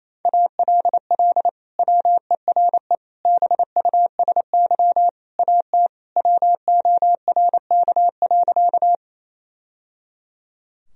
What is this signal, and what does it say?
Morse code, heard as a keyed tone, about 27 wpm, 700 Hz. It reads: ALL WERE BUSY AT WORK.
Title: Sentences from Top 1000 Words - Review Part 3 - 28wpm